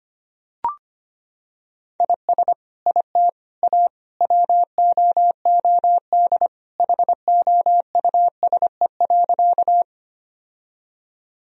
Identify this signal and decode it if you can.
Morse code, heard as a keyed tone, about 25 wpm, 700 Hz. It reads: IS IT A WOOD HOUSE.
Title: Sentences from Top 1000 Words - Review Part 2 - 25wpm